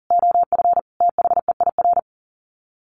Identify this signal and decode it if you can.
Morse code, heard as a keyed tone, about 40 wpm, 700 Hz. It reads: OF THEIR
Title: Sets of 5 Words Encore2 40wpm